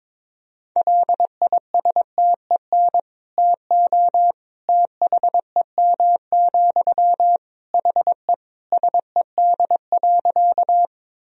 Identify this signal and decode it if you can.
Morse code, heard as a keyed tone, about 21 wpm, 700 Hz. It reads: LISTEN TO THEM, HE SED.
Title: Sentences from Top 400 Words and Abbreviations 22wpm